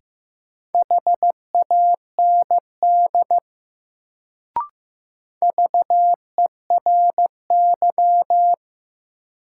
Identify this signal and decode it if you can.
Morse code, heard as a keyed tone, about 15 wpm, 700 Hz. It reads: HAND VERY
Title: Sentences from Top 200 Words - New Word Drill - 15wpm